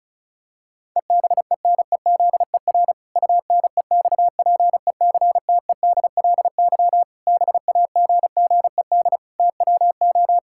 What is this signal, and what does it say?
Morse code, heard as a keyed tone, about 34 wpm, 700 Hz. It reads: EBENEZER UNEXPECTEDLY BAGGED TWO
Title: Pangram Challenge v2 - 35wpm